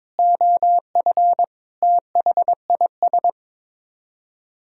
Morse code, heard as a keyed tone, about 22 wpm, 700 Hz. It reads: OF THIS
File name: Sets of 9 Words - 22wpm